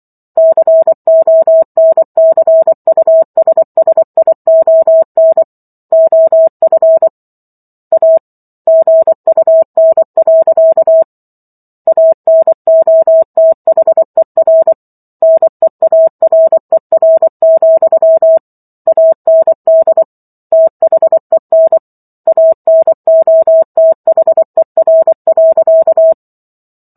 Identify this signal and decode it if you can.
Morse code, heard as a keyed tone, about 24 wpm, 650 Hz. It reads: CONCUSSION OF A GUN. ANOTHER NEARER, AND THEN ANOTHER.